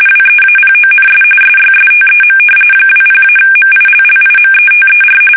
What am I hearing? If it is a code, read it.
RTTY, about 45.45 baud. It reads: WEAPONS SYSTEM 20-MILLIMETER
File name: fsk850 / battleshipnj